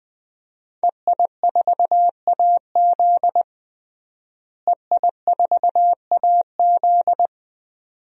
Morse code, heard as a keyed tone, about 21 wpm, 700 Hz. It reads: EI4AZ EI4AZ
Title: DXCC-100 Call Signs - 5 Characters - Repeated 2x - 20wpm